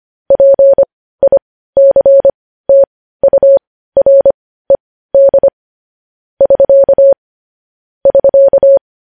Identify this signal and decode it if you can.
Morse code, heard as a keyed tone, about 25 wpm, spaced as for 15 wpm, 550 Hz. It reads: PICTURED <SK> <SK>
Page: News headlines in Morse code at 15 wpm